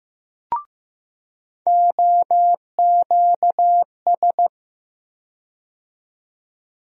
Morse code, heard as a keyed tone, about 15 wpm, 700 Hz. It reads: OQS
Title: USA Call Sign Postfixes - 3 Letters 15wpm